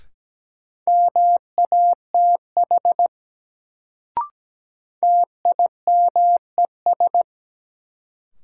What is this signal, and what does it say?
Morse code, heard as a keyed tone, about 17 wpm, 700 Hz. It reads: MATH TIMES
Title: Sentences from Top 1000 Words - Review Part 6 - New Word Drill - 17wpm